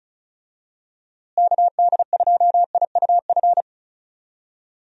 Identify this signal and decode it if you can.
Morse code, heard as a keyed tone, about 34 wpm, 700 Hz. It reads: KD2IUF
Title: USA Call Signs - 6 Characters - Repeated 2x - 35wpm